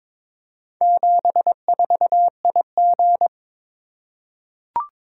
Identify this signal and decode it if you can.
Morse code, heard as a keyed tone, about 22 wpm, 700 Hz. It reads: 74IG
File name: practice / 22wpm / Four Letter-Number Combinations - 22wpm